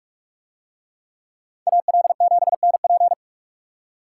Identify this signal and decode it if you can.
Morse code, heard as a keyed tone, about 41 wpm, 700 Hz. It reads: AL7NP